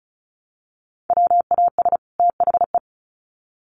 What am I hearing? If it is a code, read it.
Morse code, heard as a keyed tone, about 34 wpm, 700 Hz. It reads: WAS THE
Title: Sets of 2 Words 35wpm